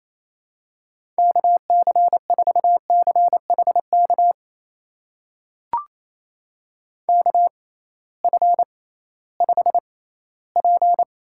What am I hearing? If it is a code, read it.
Morse code, heard as a keyed tone, about 28 wpm, spaced as for 16 wpm, 700 Hz. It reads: KC4CHK KF5P
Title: USA Call Signs - 6 Characters - 6x Character Spacing - 28wpm